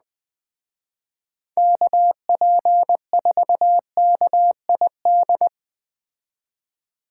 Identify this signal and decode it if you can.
Morse code, heard as a keyed tone, about 20 wpm, 700 Hz. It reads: KP4KID
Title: Speed Racing - USA 2x3 Call Signs 17wpm